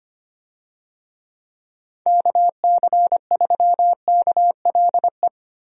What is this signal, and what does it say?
Morse code, heard as a keyed tone, about 25 wpm, 700 Hz. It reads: KC3KLE
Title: USA Call Signs - 6 Characters - Repeated 3x - 25wpm